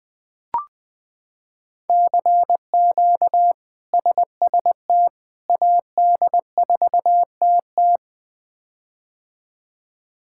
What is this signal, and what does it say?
Morse code, heard as a keyed tone, about 20 wpm, 700 Hz. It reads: CQ SST AD4TT
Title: K1USN SST QSO Practice - 20wpm